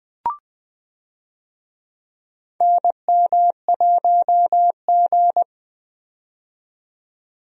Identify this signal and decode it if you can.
Morse code, heard as a keyed tone, about 20 wpm, 700 Hz. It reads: NM1G
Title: USA Call Signs - 4 Characters - 20wpm